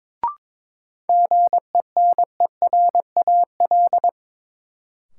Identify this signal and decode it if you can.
Morse code, heard as a keyed tone, about 22 wpm, 700 Hz. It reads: GENERAL
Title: Sentences from Top 600 Words - New Word Drill - 22wpm